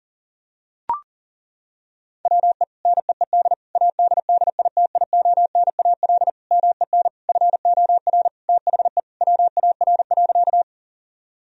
Morse code, heard as a keyed tone, about 40 wpm, 700 Hz. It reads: WE NEED ADDITIONAL MEN FOR THE WAR.
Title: Sentences from Top 1000 Words - Review Part 4 - 40wpm